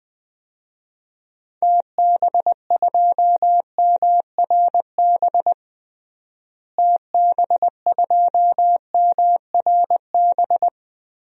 Morse code, heard as a keyed tone, about 20 wpm, 700 Hz. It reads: TB2MRB TB2MRB
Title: DXCC-100 Call Signs - 6 Characters - Repeated 2x - 20wpm